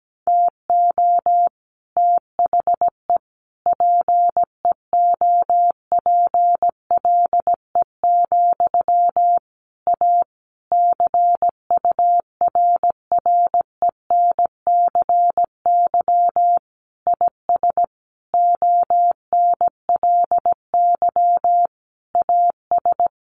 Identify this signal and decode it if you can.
Morse code, heard as a keyed tone, about 17 wpm, 700 Hz. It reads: TO THE PEOPLE, A CURRENCY IS ONLY AS
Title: Sentences from Top 1500 Words 17wpm